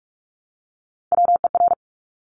Morse code, heard as a keyed tone, about 46 wpm, 700 Hz. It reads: WER
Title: Top Three Letter Combinations in Words 45wpm